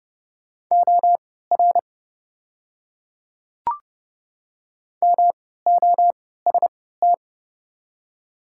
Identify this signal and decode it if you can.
Morse code, heard as a keyed tone, about 30 wpm, spaced as for 17 wpm, 700 Hz. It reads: OR MOST
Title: 3x Character Spacing - Top 100 Words - 30wpm